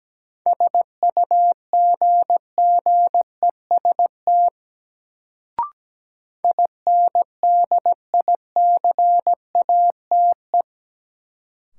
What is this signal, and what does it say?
Morse code, heard as a keyed tone, about 17 wpm, 700 Hz. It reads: SUGGEST INDICATE
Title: Sentences from Top 1300 Words - New Word Drill - 17wpm